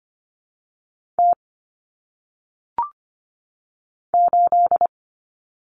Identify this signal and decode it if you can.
Morse code, heard as a keyed tone, about 24 wpm, 700 Hz. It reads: T 8
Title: Common Cut Numbers - 1-Digit 25wpm